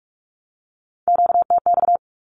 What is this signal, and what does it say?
Morse code, heard as a keyed tone, about 44 wpm, 700 Hz. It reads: QTX
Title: Q-Codes 45wpm